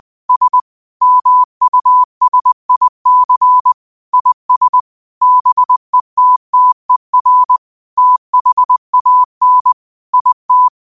Morse code, heard as a keyed tone, about 20 wpm, 1000 Hz. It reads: S MUSIC IS BETTER THAN IT
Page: Quotes for Sat, 16 Aug 2025 in Morse Code at 20 words per minute.